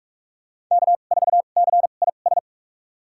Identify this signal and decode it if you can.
Morse code, heard as a keyed tone, about 46 wpm, spaced as for 33 wpm, 700 Hz. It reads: K4CIS
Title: USA Call Signs - 2x Character Spacing - 45wpm